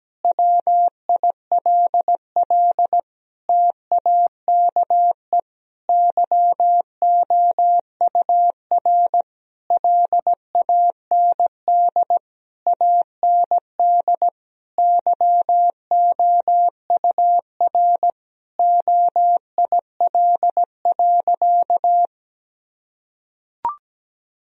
Morse code, heard as a keyed tone, about 17 wpm, 700 Hz. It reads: WILL TAKE YOUR LAND AND YOUR OIL.
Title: Sentences from Top 1000 Words - Review Part 4 - 17wpm